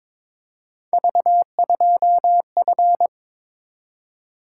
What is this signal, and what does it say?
Morse code, heard as a keyed tone, about 22 wpm, 700 Hz. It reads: V2F
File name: practice / 22wpm / Three Letter-Number Combinations - 22wpm